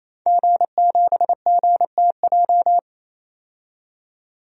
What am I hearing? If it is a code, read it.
Morse code, heard as a keyed tone, about 28 wpm, 700 Hz. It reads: G7GTJ
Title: DXCC-100 Call Signs - 5 Characters - 28wpm